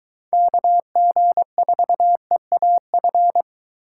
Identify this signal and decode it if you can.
Morse code, heard as a keyed tone, about 22 wpm, 700 Hz. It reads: KG4EAF